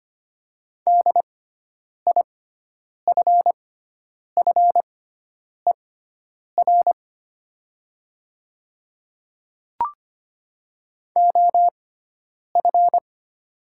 Morse code, heard as a keyed tone, about 25 wpm, spaced as for 9 wpm, 700 Hz. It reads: DIFFER OF
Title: 6x Character Spacing - Top 200 Words - 25wpm